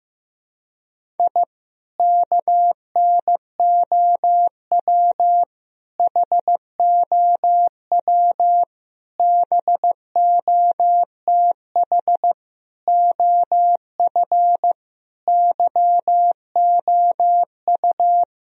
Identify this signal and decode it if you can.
Morse code, heard as a keyed tone, about 15 wpm, 700 Hz. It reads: I KNOW HOW BOTH OF YOU
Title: Sentences from Top 1000 Words - Review Part 1 - 15wpm